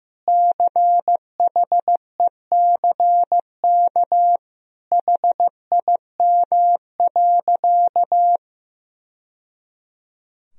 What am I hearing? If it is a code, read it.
Morse code, heard as a keyed tone, about 15 wpm, 700 Hz. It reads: CHECK HIM.
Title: Sentences from Top 1000 Words - Review Part 3 - 15wpm